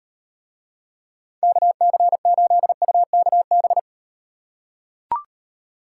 Morse code, heard as a keyed tone, about 38 wpm, 700 Hz. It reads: KC8UKB